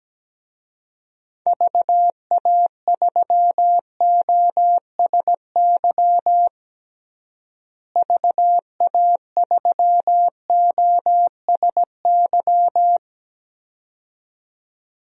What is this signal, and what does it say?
Morse code, heard as a keyed tone, about 17 wpm, 700 Hz. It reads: VA3OSY VA3OSY
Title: DXCC-100 Call Signs - 6 Characters - Repeated 2x - 17wpm